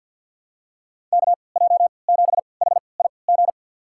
Morse code, heard as a keyed tone, about 53 wpm, spaced as for 29 wpm, 700 Hz. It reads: KJ7HIG
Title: USA Call Signs - 3x Character Spacing - 50wpm